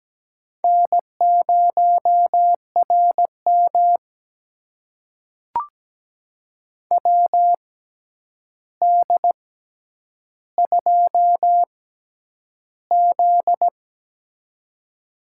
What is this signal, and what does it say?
Morse code, heard as a keyed tone, about 17 wpm, spaced as for 9 wpm, 700 Hz. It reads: N0RM WD2Z
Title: USA Call Signs - 4 Characters - 6x Character Spacing - 17wpm